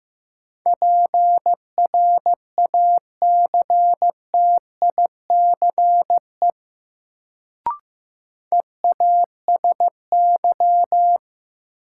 Morse code, heard as a keyed tone, about 15 wpm, 700 Hz. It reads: PRACTICE EASY
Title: Sentences from Top 700 Words - New Word Drill - 15wpm